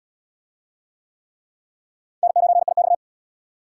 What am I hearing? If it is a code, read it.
Morse code, heard as a keyed tone, about 77 wpm, 700 Hz. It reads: N9IX